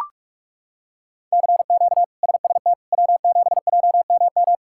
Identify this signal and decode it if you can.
Morse code, heard as a keyed tone, about 41 wpm, 700 Hz. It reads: CQ SST W7JMM
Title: K1USN SST QSO Practice - 45wpm